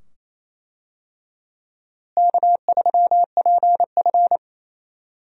Morse code, heard as a keyed tone, about 28 wpm, 700 Hz. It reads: K3PF
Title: USA Call Signs - 4 Characters - Repeated 2x - 28wpm